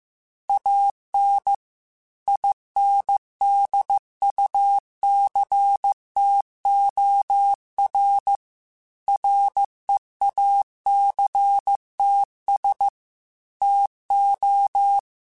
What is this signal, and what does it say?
Morse code, heard as a keyed tone, about 15 wpm, 800 Hz. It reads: AN INDUCTOR REACTS TO
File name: Technical Subjects / Technical Subjects-mp3s / XL @13